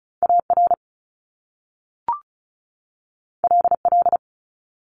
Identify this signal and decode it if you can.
Morse code, heard as a keyed tone, about 36 wpm, 700 Hz. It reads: AR LL
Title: Top Two Letter Combinations in Words 35wpm